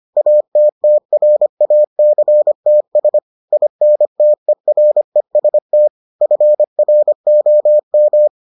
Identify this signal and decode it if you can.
Morse code, heard as a keyed tone, about 25 wpm, 600 Hz. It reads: ATTRACTS INTEREST FROM